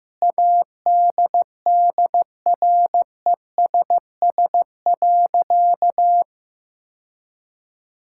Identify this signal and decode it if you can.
Morse code, heard as a keyed tone, about 16 wpm, 700 Hz. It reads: ADDRESS.
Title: Sentences from Top 1000 Words - Review Part 6 - 15wpm